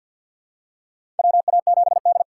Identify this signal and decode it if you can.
Morse code, heard as a keyed tone, about 52 wpm, 700 Hz. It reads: WA7D
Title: USA Call Signs - 3x Character Spacing - 50wpm